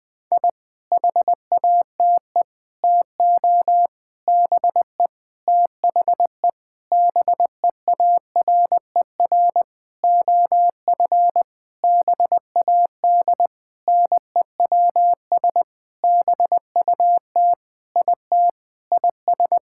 Morse code, heard as a keyed tone, about 20 wpm, 700 Hz. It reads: I HATE TO BE THE BEARER OF BAD NEWS BUT IT IS
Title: Sentences with 9-Word Phrases - 20wpm